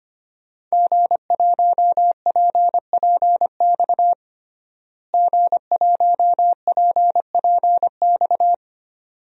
Morse code, heard as a keyed tone, about 25 wpm, 700 Hz. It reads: G1PPX G1PPX
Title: DXCC-100 Call Signs - 5 Characters - Repeated 2x - 25wpm